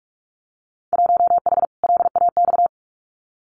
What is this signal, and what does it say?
Morse code, heard as a keyed tone, about 46 wpm, 700 Hz. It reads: 1H LAX
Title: Winter Field Day Exchanges 45wpm